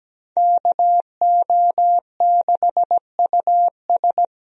Morse code, heard as a keyed tone, about 17 wpm, 700 Hz. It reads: KO6US
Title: USA Call Signs - 5 Characters - 17wpm